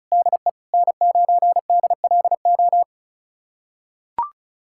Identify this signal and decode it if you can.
Morse code, heard as a keyed tone, about 34 wpm, 700 Hz. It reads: DE N9DLO
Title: USA Callsign DE Callsign - 5 Characters - 35wpm